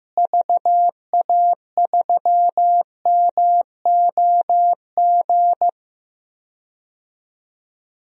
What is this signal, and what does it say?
Morse code, heard as a keyed tone, about 15 wpm, 700 Hz. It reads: VA3MOG
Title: DXCC-100 Call Signs - 6 Characters - Repeated 2x - 15wpm